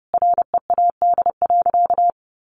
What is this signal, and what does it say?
Morse code, heard as a keyed tone, about 30 wpm, 700 Hz. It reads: READ.
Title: Sentences from Top 1100 Words 30wpm